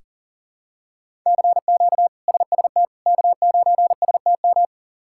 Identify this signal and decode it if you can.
Morse code, heard as a keyed tone, about 40 wpm, 700 Hz. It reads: CQ SST K9STM
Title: K1USN SST QSO Practice - 40wpm